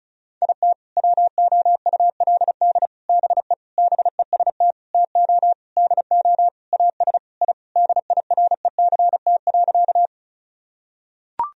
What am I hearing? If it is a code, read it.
Morse code, heard as a keyed tone, about 36 wpm, 700 Hz. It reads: IT WOULD BE BEST TO DO AS I DIRECT.
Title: Sentences from Top 1000 Words - Review Part 2 - 35wpm